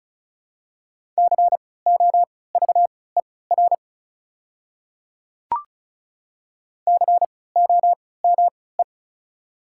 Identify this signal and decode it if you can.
Morse code, heard as a keyed tone, about 36 wpm, spaced as for 20 wpm, 700 Hz. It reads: COVER COME
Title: 3x Character Spacing - Top 300 Words - 35wpm